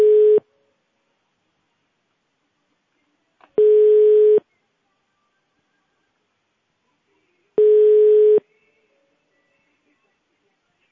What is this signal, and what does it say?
Ringback tone (425 Hz, 1 s on, 3 s off).